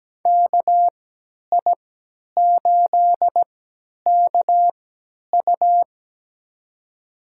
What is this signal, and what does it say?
Morse code, heard as a keyed tone, about 17 wpm, spaced as for 10 wpm, 700 Hz. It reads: KI8KU
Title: USA Call Signs - 5 Characters - 3x Character Spacing - 17wpm